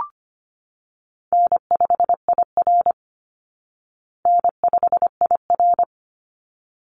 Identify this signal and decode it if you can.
Morse code, heard as a keyed tone, about 25 wpm, 700 Hz. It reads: N5IR N5IR
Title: Easy Call Signs Repeated 2x - 25wpm